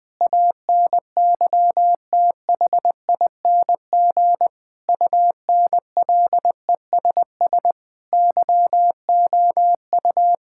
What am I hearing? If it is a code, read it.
Morse code, heard as a keyed tone, about 20 wpm, 700 Hz. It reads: ANYTHING UNLESS YOU
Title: Sentences from Top 1000 Words - Review Part 6 - 20wpm